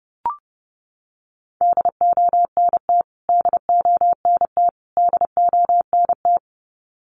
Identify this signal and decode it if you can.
Morse code, heard as a keyed tone, about 30 wpm, 700 Hz. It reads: DONT DONT DONT
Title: Top 500 Words Repeated 30wpm